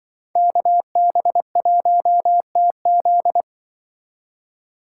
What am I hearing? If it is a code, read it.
Morse code, heard as a keyed tone, about 24 wpm, 700 Hz. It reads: KB1TZ
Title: Speed Racing - USA 2x2 Call Signs 20wpm